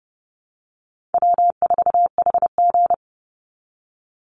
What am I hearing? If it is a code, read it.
Morse code, heard as a keyed tone, about 30 wpm, 700 Hz. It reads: W4HG
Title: Easy Call Signs Repeated 2x - 30wpm